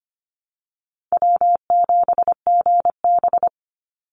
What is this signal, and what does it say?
Morse code, heard as a keyed tone, about 25 wpm, 700 Hz. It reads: W7GB
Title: Call Signs - 25wpm